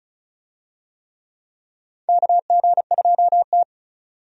Morse code, heard as a keyed tone, about 34 wpm, 700 Hz. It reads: KG2T
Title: USA Call Signs - 4 Characters - 35wpm